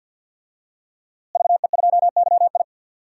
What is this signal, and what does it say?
Morse code, heard as a keyed tone, about 49 wpm, 700 Hz. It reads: VE2YI
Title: DXCC-100 Call Signs - 5 Characters - Repeated 3x - 50wpm